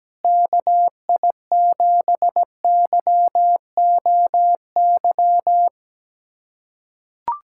Morse code, heard as a keyed tone, about 17 wpm, 700 Hz. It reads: KI7YOY
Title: USA Callsign DE Callsign - 6 Characters - 17wpm